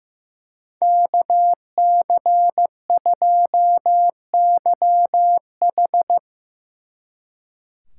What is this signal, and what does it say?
Morse code, heard as a keyed tone, about 16 wpm, 700 Hz. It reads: KC2YH
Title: Speed Racing - USA 2x2 Call Signs 15wpm